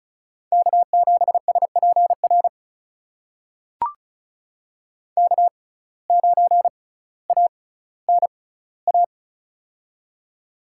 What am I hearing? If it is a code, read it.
Morse code, heard as a keyed tone, about 36 wpm, spaced as for 18 wpm, 700 Hz. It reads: K7SPR K9ANA
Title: USA Call Signs - 5 Characters - 6x Character Spacing - 35wpm